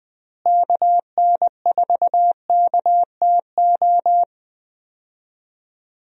Morse code, heard as a keyed tone, about 19 wpm, 700 Hz. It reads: KN4KTO
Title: USA Call Signs - 6 Characters - Repeated 3x - 20wpm